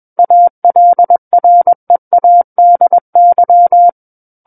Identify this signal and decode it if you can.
Morse code, heard as a keyed tone, about 21 wpm, 700 Hz. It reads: ALREADY
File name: Book One / 12-Chapter 12 - 21 WPM